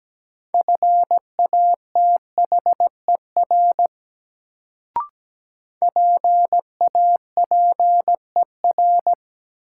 Morse code, heard as a keyed tone, about 17 wpm, 700 Hz. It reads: FATHER PAPER
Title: Sentences from Top 300 Words - New Word Drill - 17wpm